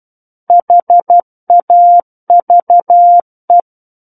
Morse code, heard as a keyed tone, about 12 wpm, 700 Hz. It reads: HAVE
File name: Book One / 03-Chapter 3 - 12 WPM